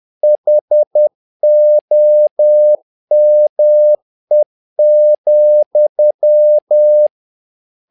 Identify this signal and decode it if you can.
Morse code, heard as a keyed tone, about 10 wpm, 600 Hz. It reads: HOME,
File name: learning center / tower trouble / tower trouble 10wpm